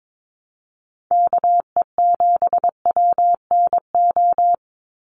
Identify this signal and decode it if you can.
Morse code, heard as a keyed tone, about 22 wpm, 700 Hz. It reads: KE7WNO
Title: USA Call Signs 22wpm